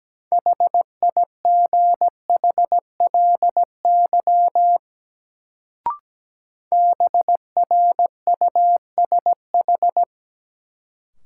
Morse code, heard as a keyed tone, about 17 wpm, 700 Hz. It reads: HIGHLY BRUSH
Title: Sentences from Top 1100 Words - New Word Drill - 17wpm